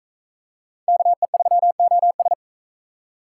Morse code, heard as a keyed tone, about 41 wpm, 700 Hz. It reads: KE3OS